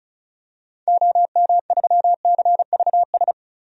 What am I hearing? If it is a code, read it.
Morse code, heard as a keyed tone, about 36 wpm, 700 Hz. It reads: OM3CVS
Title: DXCC-100 Call Signs - 6 Characters - Repeated 3x - 35wpm